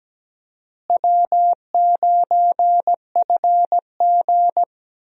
Morse code, heard as a keyed tone, about 17 wpm, 700 Hz. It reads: W9FG
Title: USA Call Signs - 4 Characters - Repeated 2x - 17wpm